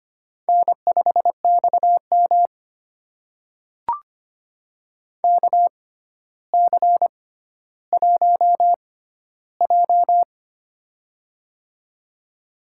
Morse code, heard as a keyed tone, about 25 wpm, spaced as for 13 wpm, 700 Hz. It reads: N5XM KC1J
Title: USA Call Signs - 4 Characters - 6x Character Spacing - 25wpm